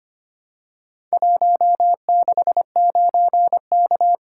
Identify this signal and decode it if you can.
Morse code, heard as a keyed tone, about 25 wpm, 700 Hz. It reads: 169K
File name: practice / 25wpm / Four Letter-Number Combinations - 25wpm